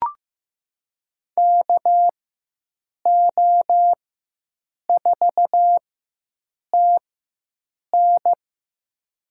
Morse code, heard as a keyed tone, about 15 wpm, spaced as for 7 wpm, 700 Hz. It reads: KO4TN
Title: USA Call Signs - 5 Characters - 4x Character Spacing - 15wpm